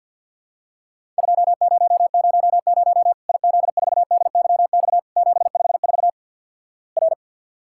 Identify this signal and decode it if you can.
Morse code, heard as a keyed tone, about 48 wpm, 700 Hz. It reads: 2000 IZ4DYX 754 R
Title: Worked All Europe DX Contest - QTC Exchange with Repeats - 50wpm